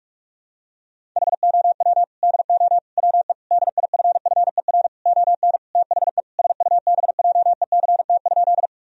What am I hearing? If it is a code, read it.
Morse code, heard as a keyed tone, about 43 wpm, 700 Hz. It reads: HOW DO WE DIFFER ON THE SUBJECT?
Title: Sentences from Top 1000 Words - Review Part 1 - 45wpm